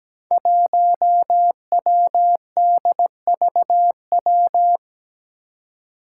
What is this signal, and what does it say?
Morse code, heard as a keyed tone, about 17 wpm, 700 Hz. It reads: 1WDVW
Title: Five Challenging Letter-Number Combinations 17wpm